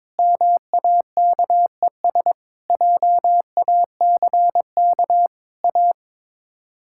Morse code, heard as a keyed tone, about 22 wpm, 700 Hz. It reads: MAKES JACK A
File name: practice / 22wpm / Sets of 7 Words - 22wpm